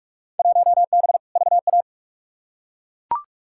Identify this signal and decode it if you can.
Morse code, heard as a keyed tone, about 42 wpm, 700 Hz. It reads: 1B VA
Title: Field Day Exchanges Encore - 45wpm